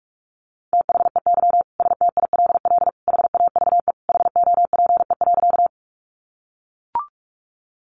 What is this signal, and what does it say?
Morse code, heard as a keyed tone, about 44 wpm, 700 Hz. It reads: THEY STILL HAVE HOPE.
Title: Sentences from Top 1000 Words 45wpm